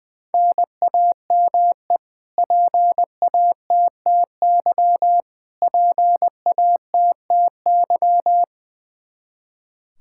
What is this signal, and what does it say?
Morse code, heard as a keyed tone, about 21 wpm, 700 Hz. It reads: NAME PATTY PATTY
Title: Practice QSO - Exchange 1 - Beginning - 20wpm